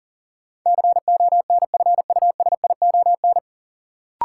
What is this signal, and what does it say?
Morse code, heard as a keyed tone, about 40 wpm, 700 Hz. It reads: CONFUSION
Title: Sentences from Top 1600 Words - New Word Drill Alt - 40wpm